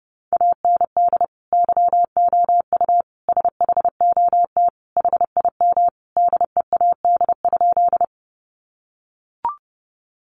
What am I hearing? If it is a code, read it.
Morse code, heard as a keyed tone, about 28 wpm, 700 Hz. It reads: AND YOU SHOT HIM DEAD?
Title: Sentences from Top 1500 Words 30wpm